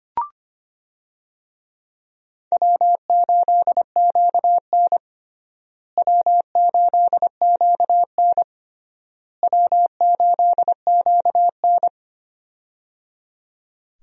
Morse code, heard as a keyed tone, about 25 wpm, 700 Hz. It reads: W8QN W8QN W8QN
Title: USA Call Signs - 4 Characters - Repeated 3x - 25wpm